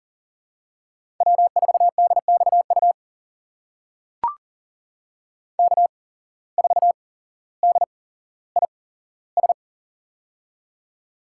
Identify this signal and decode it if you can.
Morse code, heard as a keyed tone, about 42 wpm, spaced as for 17 wpm, 700 Hz. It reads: W4DXU K4DIS
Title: USA Call Signs - 5 Characters - 8x Character Spacing - 40wpm